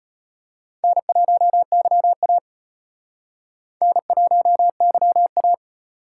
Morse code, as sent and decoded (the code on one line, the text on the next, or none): -. .---- -.-- .- / -. .---- -.-- .-
N1YA N1YA